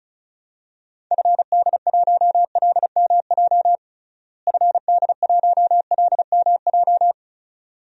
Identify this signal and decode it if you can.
Morse code, heard as a keyed tone, about 34 wpm, 700 Hz. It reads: FD1LMJ FD1LMJ
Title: DXCC-100 Call Signs - 6 Characters - Repeated 2x - 35wpm